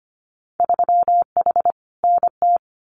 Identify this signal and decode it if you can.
Morse code, heard as a keyed tone, about 25 wpm, 700 Hz. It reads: 3H NT
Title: Winter Field Day Exchanges Encore 25wpm